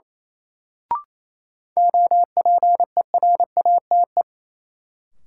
Morse code, heard as a keyed tone, about 28 wpm, 700 Hz. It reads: OPERATE
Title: Sentences from Top 1500 Words - New Word Drill Alt - 28wpm